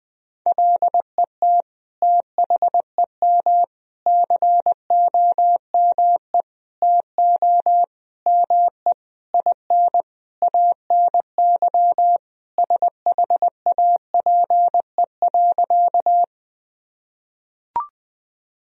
Morse code, heard as a keyed tone, about 20 wpm, 700 Hz. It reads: LET THEM COME TO ME IN ANY SHAPE.
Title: Sentences from Top 1000 Words - Review Part 3 - 20wpm